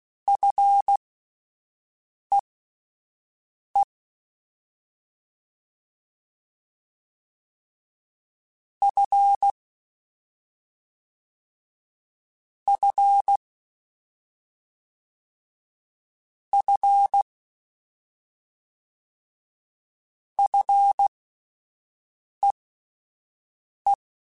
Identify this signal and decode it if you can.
Morse code, heard as a keyed tone, about 16 wpm, spaced as for 5 wpm, 800 Hz. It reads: FEE F F F FEE
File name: Fs / Single F mp3s / Single F Prefixes 3x2 Twice @5